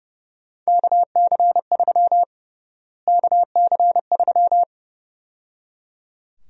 Morse code, heard as a keyed tone, about 30 wpm, 700 Hz. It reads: KC3 KC3
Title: USA Call Sign Prefixes - Repeated 2x - 30wpm